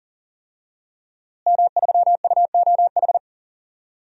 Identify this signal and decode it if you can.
Morse code, heard as a keyed tone, about 40 wpm, 700 Hz. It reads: M3UOH
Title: DXCC-100 Call Signs - 5 Characters - Repeated 2x - 40wpm